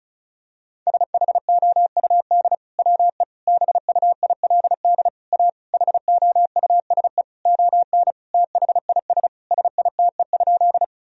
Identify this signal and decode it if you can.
Morse code, heard as a keyed tone, about 36 wpm, 700 Hz. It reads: SHOUD WE BUILD A HOUSE ON THIS SITE?
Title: Sentences from Top 1000 Words - Review Part 4 - 35wpm